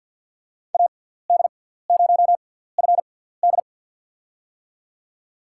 Morse code, heard as a keyed tone, about 47 wpm, spaced as for 17 wpm, 700 Hz. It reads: AD0FD
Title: USA Call Signs - 5 Characters - 6x Character Spacing - 50wpm